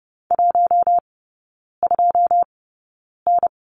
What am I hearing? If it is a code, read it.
Morse code, heard as a keyed tone, about 30 wpm, spaced as for 9 wpm, 700 Hz. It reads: 12N
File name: Lesson 18e - Letter m - Three Characters 30wpm